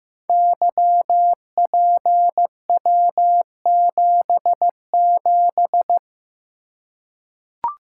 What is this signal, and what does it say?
Morse code, heard as a keyed tone, about 15 wpm, 700 Hz. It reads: YPW77
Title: Code Groups - 5 Letter:Numbers - 15wpm